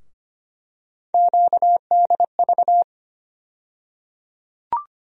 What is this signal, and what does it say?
Morse code, heard as a keyed tone, about 25 wpm, 700 Hz. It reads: QDV
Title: USA Call Sign Postfixes - 3 Letters 25wpm